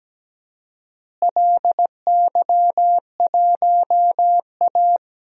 Morse code, heard as a keyed tone, about 17 wpm, 700 Hz. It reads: LY1A
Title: DXCC-100 Call Signs - 4 Characters - Repeated 2x - 17wpm